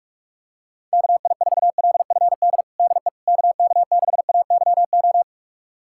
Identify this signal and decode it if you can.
Morse code, heard as a keyed tone, about 44 wpm, 700 Hz. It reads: KI4LFD DE KK6AYO
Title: USA Callsign DE Callsign 45wpm